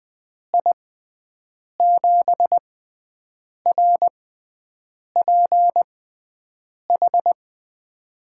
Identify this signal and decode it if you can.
Morse code, heard as a keyed tone, about 21 wpm, spaced as for 7 wpm, 700 Hz. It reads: I7RPH
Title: USA Call Signs - 6 Characters - 6x Character Spacing - 20wpm